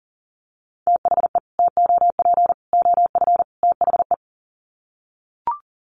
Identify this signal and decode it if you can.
Morse code, heard as a keyed tone, about 40 wpm, 700 Hz. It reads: THE TOP OF THE
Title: Sets of 4 Words 40wpm